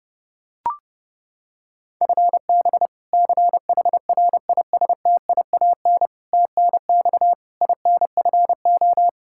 Morse code, heard as a keyed tone, about 30 wpm, 700 Hz. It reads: FB CHRISTIAN TNX INFO
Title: K1USN SST QSO Practice 30wpm